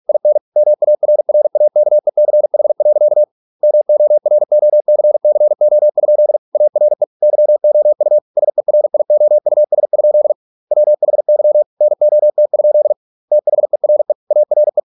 Morse code, heard as a keyed tone, about 46 wpm, 600 Hz. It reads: IN MARRAKECH. MOROCCO? ARE YOU SERIOUS? WHY NOT? THERE ARE